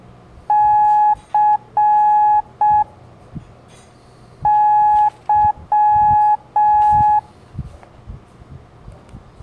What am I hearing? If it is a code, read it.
Morse code, heard as a keyed tone, about 6 wpm, 840 Hz. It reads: C Y
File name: blog / LETRAS COM SOM SEMELHANTE MORSE